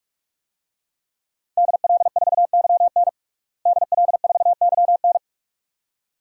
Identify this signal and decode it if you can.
Morse code, heard as a keyed tone, about 42 wpm, 700 Hz. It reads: DL4YN DL4YN
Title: DXCC-100 Call Signs - 5 Characters - Repeated 2x - 45wpm